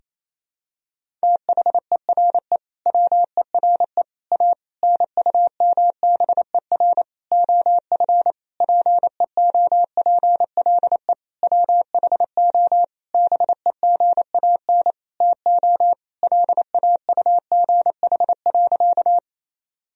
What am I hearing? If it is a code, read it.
Morse code, heard as a keyed tone, about 28 wpm, 700 Hz. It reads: THERE WERE A NUMBER OF PEOPLE WHO BEGAN TO LAUGH.
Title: Sentences from Top 1000 Words - Review Part 1 - 28wpm